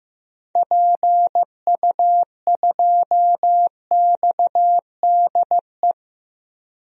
Morse code, heard as a keyed tone, about 15 wpm, 700 Hz. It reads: PU2XDE
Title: DXCC-100 Call Signs - 6 Characters - 15wpm